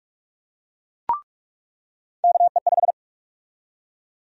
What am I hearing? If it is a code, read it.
Morse code, heard as a keyed tone, about 46 wpm, 700 Hz. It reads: KE5
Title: USA Call Sign Prefixes - 45wpm